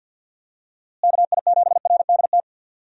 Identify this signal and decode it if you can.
Morse code, heard as a keyed tone, about 52 wpm, 700 Hz. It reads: KI7RDT
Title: USA Call Signs - 6 Characters - Repeated 2x - 50wpm